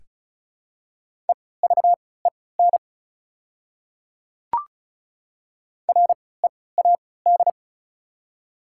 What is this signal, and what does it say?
Morse code, heard as a keyed tone, about 34 wpm, spaced as for 20 wpm, 700 Hz. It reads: EVEN READ
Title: 3x Character Spacing - Top 200 Words - 35wpm